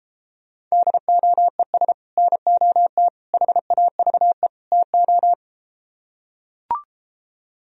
Morse code, heard as a keyed tone, about 33 wpm, 700 Hz. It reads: DOES NOT HAVE TO